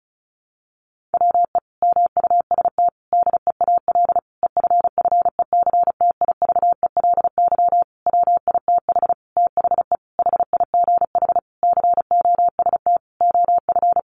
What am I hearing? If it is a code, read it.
Morse code, heard as a keyed tone, about 34 wpm, 700 Hz. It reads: WE MUST DEAL EFFECTIVELY WITH THE HIGH COST OF
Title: Sentences from Top 1400 Words 35wpm